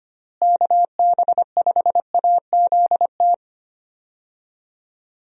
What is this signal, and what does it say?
Morse code, heard as a keyed tone, about 25 wpm, 700 Hz. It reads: KB5AZT
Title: USA Call Signs - 6 Characters - Repeated 3x - 25wpm